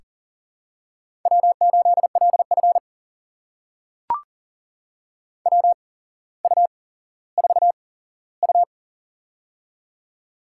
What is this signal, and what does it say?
Morse code, heard as a keyed tone, about 42 wpm, spaced as for 17 wpm, 700 Hz. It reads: W8LF WU4U
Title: USA Call Signs - 4 Characters - 8x Character Spacing - 40wpm